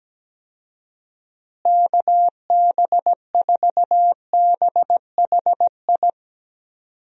Morse code, heard as a keyed tone, about 17 wpm, 700 Hz. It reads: KB4BHI
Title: USA Call Signs - 6 Characters - Repeated 2x - 17wpm